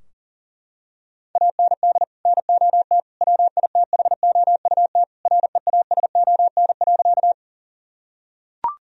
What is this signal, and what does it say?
Morse code, heard as a keyed tone, about 40 wpm, 700 Hz. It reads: AND NOT WITHOUT REASON.
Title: Sentences from Top 1000 Words - Review Part 3 - 40wpm